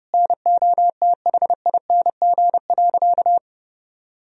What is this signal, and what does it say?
Morse code, heard as a keyed tone, about 30 wpm, 700 Hz. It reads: NOTHING.
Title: Sentences from Top 1000 Words - Review Part 2 - 30wpm